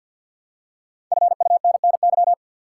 Morse code, heard as a keyed tone, about 52 wpm, 700 Hz. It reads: FUNNY
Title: Five Letter Words - 50wpm